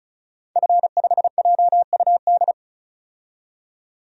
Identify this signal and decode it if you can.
Morse code, heard as a keyed tone, about 36 wpm, 700 Hz. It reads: F5JUD
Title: DXCC-100 Call Signs - 5 Characters - 35wpm